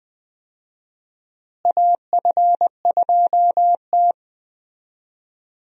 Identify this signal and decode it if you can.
Morse code, heard as a keyed tone, about 20 wpm, 700 Hz. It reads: AF2T
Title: USA Call Signs - 4 Characters - Repeated 3x - 20wpm